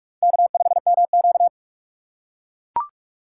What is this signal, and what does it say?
Morse code, heard as a keyed tone, about 42 wpm, 700 Hz. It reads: K5MQ
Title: USA Call Signs - 4x Character Spacing - 45wpm